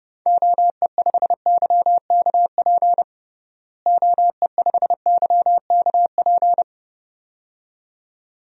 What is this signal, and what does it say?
Morse code, heard as a keyed tone, about 29 wpm, 700 Hz. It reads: OE5YKP OE5YKP
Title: DXCC-100 Call Signs - 6 Characters - Repeated 2x - 30wpm